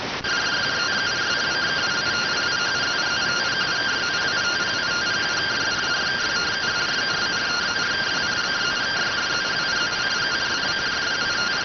Two-tone teleprinter signal, about 170 Hz shift.